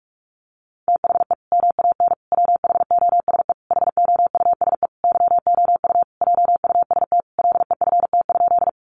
Morse code, heard as a keyed tone, about 44 wpm, 700 Hz. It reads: THE MAN WHOSE HOUSE YOU JUST LEFT?
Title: Sentences from Top 1200 Words 45wpm